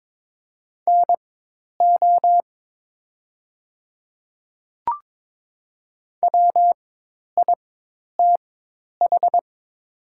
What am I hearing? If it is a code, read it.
Morse code, heard as a keyed tone, about 22 wpm, spaced as for 10 wpm, 700 Hz. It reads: NO WITH
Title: 4x Character Spacing - Top 100 Words - 22wpm